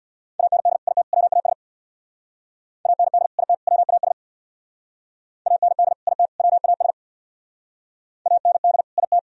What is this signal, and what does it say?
Morse code, heard as a keyed tone, about 64 wpm, 700 Hz. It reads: AND IT WAS AND IT WAS AND IT WAS AND IT